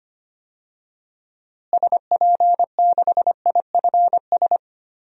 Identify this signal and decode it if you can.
Morse code, heard as a keyed tone, about 25 wpm, 700 Hz. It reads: SP6IFS
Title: DXCC-100 Call Signs - 6 Characters - Repeated 2x - 25wpm